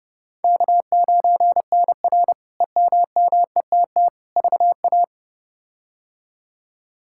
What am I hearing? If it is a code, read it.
Morse code, heard as a keyed tone, about 31 wpm, 700 Hz. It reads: K9NR EMMETT VA
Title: K1USN SST QSO Practice - 30wpm